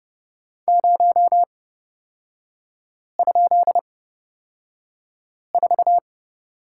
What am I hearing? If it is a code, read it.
Morse code, heard as a keyed tone, about 30 wpm, spaced as for 5 wpm, 700 Hz. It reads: 0?4
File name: Lesson 36e - BT - Three Characters 30wpm